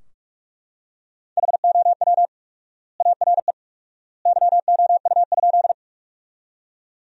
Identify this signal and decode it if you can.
Morse code, heard as a keyed tone, about 41 wpm, 700 Hz. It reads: HOW ARE YOU?
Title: 4x Word-Spacing - Sentences from Top 100 Words - 45wpm